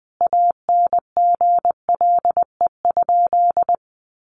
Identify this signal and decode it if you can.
Morse code, heard as a keyed tone, about 20 wpm, 700 Hz. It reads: ANGLE?
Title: Sentences from Top 1500 Words 20wpm